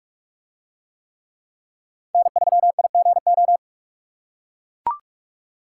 Morse code, heard as a keyed tone, about 42 wpm, 700 Hz. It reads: N3IGO